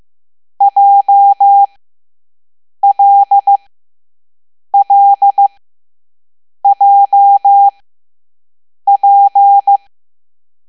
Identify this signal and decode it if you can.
Morse code, heard as a keyed tone, about 15 wpm, spaced as for 6 wpm, 800 Hz. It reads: JLLJP
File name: Morse / Level7B